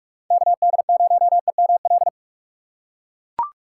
Morse code, heard as a keyed tone, about 44 wpm, 700 Hz. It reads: KD0EML
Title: USA Call Signs - 6 Characters - 6x Character Spacing - 45wpm